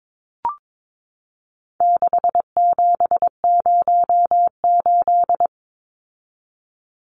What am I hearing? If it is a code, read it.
Morse code, heard as a keyed tone, about 22 wpm, 700 Hz. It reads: 6708
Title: Numbers - 4-Digit 22wpm